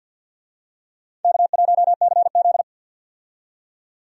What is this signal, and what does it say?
Morse code, heard as a keyed tone, about 51 wpm, 700 Hz. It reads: K1CZ
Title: USA Call Signs - 4 Characters - 50wpm